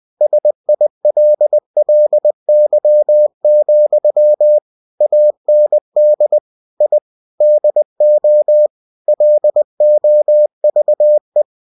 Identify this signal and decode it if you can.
Morse code, heard as a keyed tone, about 20 wpm, 600 Hz. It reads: SILLY, AND I DO LOVE
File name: cw-books / winnie-the-pooh / winnie-20wpm-page-0097